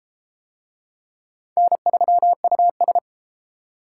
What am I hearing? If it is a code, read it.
Morse code, heard as a keyed tone, about 28 wpm, 700 Hz. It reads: N3US
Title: Speed Racing - USA 1x2 Call Signs 22wpm